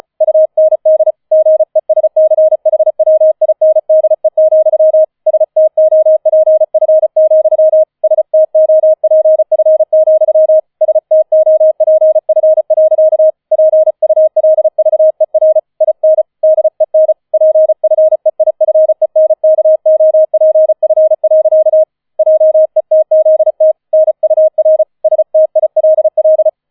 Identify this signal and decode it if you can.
Morse code, heard as a keyed tone, about 35 wpm, 630 Hz. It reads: UND GESCHWINDE, STOPF, STOPF, STOPF. PULVER IN DEN PFEIFENKOPF. JETZT NUR STILL
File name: 004 Vierter Streich 35wpm